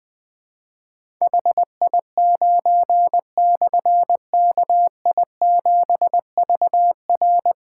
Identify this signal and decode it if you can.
Morse code, heard as a keyed tone, about 20 wpm, 700 Hz. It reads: HI9/KI7VR
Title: Difficult Call Signs 20wpm